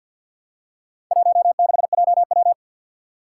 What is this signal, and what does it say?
Morse code, heard as a keyed tone, about 51 wpm, 700 Hz. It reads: 16JW
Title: Four Challenging Letter-Number Combinations 50wpm